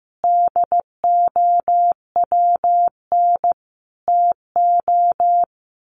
Morse code, heard as a keyed tone, about 15 wpm, 700 Hz. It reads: DOWN TO
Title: Sets of 5 Words Encore2 15wpm